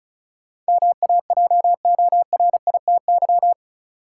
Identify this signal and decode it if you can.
Morse code, heard as a keyed tone, about 36 wpm, 700 Hz. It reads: MAJORITY
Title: Sets of 3 Words - New Word Drill - 35wpm